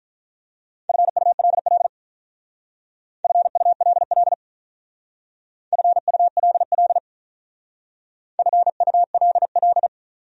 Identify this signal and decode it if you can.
Morse code, heard as a keyed tone, about 40 wpm, 700 Hz. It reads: FULL FULL FULL FULL
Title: Speed-Racing Top 500 Words 35wpm